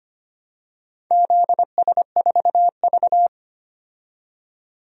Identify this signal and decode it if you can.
Morse code, heard as a keyed tone, about 25 wpm, 700 Hz. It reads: ZS4V
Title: DXCC-100 Call Signs - 4 Characters - 25wpm